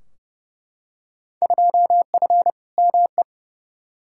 Morse code, heard as a keyed tone, about 32 wpm, 700 Hz. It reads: 2F ME
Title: Field Day Exchanges - 30wpm